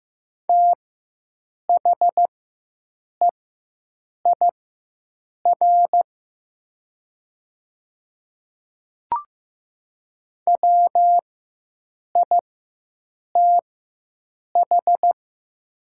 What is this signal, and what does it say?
Morse code, heard as a keyed tone, about 15 wpm, spaced as for 7 wpm, 700 Hz. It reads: THEIR WITH
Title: 4x Character Spacing - Top 100 Words - 15wpm